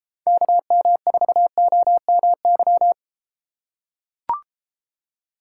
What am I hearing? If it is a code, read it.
Morse code, heard as a keyed tone, about 32 wpm, 700 Hz. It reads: KM4OMY